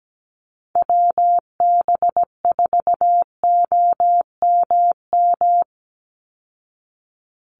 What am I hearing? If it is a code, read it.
Morse code, heard as a keyed tone, about 17 wpm, 700 Hz. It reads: WB4OMM
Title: Call Signs Repeated 2x - 17wpm